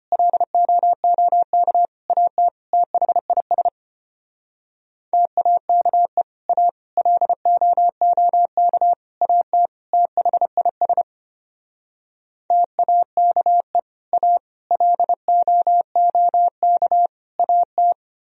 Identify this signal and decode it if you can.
Morse code, heard as a keyed tone, about 32 wpm, 700 Hz. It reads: LOOK AT THIS TAKE A LOOK AT THIS TAKE A LOOK AT